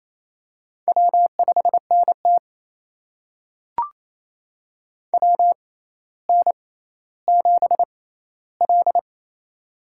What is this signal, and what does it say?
Morse code, heard as a keyed tone, about 28 wpm, spaced as for 14 wpm, 700 Hz. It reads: W5NT WN7L
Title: USA Call Signs - 4 Characters - 6x Character Spacing - 28wpm